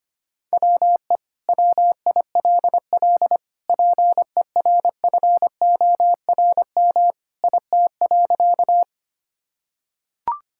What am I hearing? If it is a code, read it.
Morse code, heard as a keyed tone, about 25 wpm, 700 Hz. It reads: WE WILL PERFORM IT.
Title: Sentences from Top 1000 Words - Review Part 5 - 25wpm